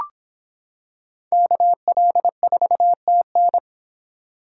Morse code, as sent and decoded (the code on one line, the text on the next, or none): -.- .-.. ....- - -.
KL4TN